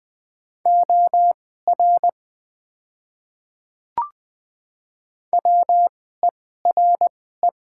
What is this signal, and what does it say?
Morse code, heard as a keyed tone, about 21 wpm, spaced as for 15 wpm, 700 Hz. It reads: OR WERE
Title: 2x Character Spacing - Top 100 Words - 20wpm